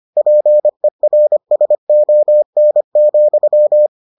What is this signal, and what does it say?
Morse code, heard as a keyed tone, about 25 wpm, 600 Hz. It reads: PERSON,